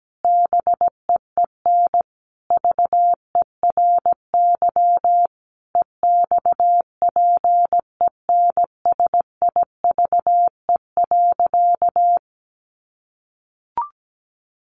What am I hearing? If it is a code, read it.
Morse code, heard as a keyed tone, about 17 wpm, 700 Hz. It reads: BEEN VERY EXPENSIVE.
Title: Sentences from Top 1500 Words 17wpm